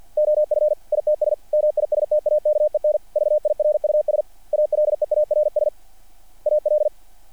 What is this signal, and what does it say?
Morse code, heard as a keyed tone, about 48 wpm, 600 Hz. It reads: OF ITS MISTAKEN VIRUS ALEARS AL
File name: archived / CRSnet